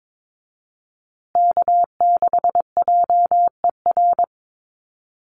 Morse code, heard as a keyed tone, about 22 wpm, 700 Hz. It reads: K6JER
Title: USA Call Signs Repeated 3x 22wpm